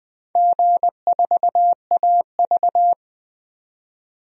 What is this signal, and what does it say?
Morse code, heard as a keyed tone, about 20 wpm, 700 Hz. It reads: G4AV
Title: DXCC-100 Call Signs - 4 Characters - 20wpm